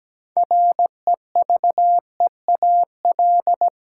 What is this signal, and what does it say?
Morse code, heard as a keyed tone, about 17 wpm, 700 Hz. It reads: REVEAL
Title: Six Letter Words - 17wpm